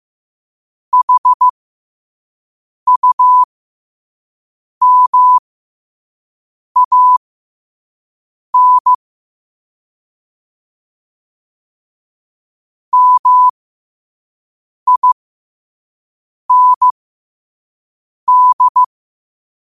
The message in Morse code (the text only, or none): HUMAN MIND